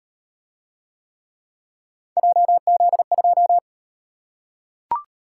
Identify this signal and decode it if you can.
Morse code, heard as a keyed tone, about 38 wpm, 700 Hz. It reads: JZ2